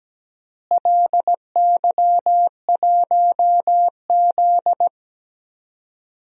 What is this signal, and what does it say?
Morse code, heard as a keyed tone, about 17 wpm, 700 Hz. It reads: LY1Z
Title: DXCC-100 Call Signs - 4 Characters - Repeated 3x - 17wpm